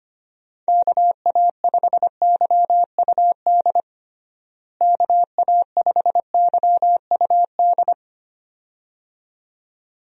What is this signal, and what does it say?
Morse code, heard as a keyed tone, about 25 wpm, 700 Hz. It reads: KA5YUD KA5YUD
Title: USA Call Signs - 6 Characters - Repeated 2x - 25wpm